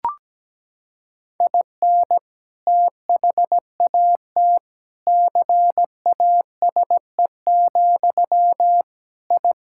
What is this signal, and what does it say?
Morse code, heard as a keyed tone, about 17 wpm, 700 Hz. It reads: IN THAT CASE, I
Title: Sentences from Top 1000 Words - Review Part 6 - 17wpm